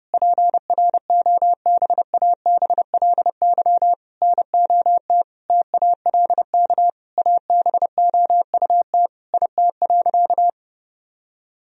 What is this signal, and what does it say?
Morse code, heard as a keyed tone, about 30 wpm, 700 Hz. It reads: PROBABLY NOT TALK ABOUT IT.
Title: Sentences from Top 1000 Words - Review Part 4 - 30wpm